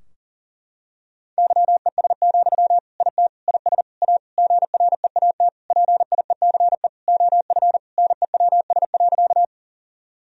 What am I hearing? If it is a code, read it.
Morse code, heard as a keyed tone, about 40 wpm, 700 Hz. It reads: YES, IT IS A GREAT PIECE OF NEWS.
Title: Sentences from Top 1000 Words - Review Part 2 - 40wpm